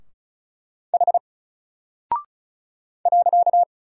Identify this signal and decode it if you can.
Morse code, heard as a keyed tone, about 36 wpm, 700 Hz. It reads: H .
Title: Single Character (Letter:Number:Punctuation:Prosign) Alt 35wpm